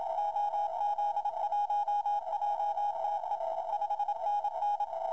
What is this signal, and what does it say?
Morse code, heard as a keyed tone, about 27 wpm, 820 Hz, with noise. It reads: OZ1JTE 5NN
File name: Lydfiler / Willis is. VK9DWX / VK9DWX 80M CW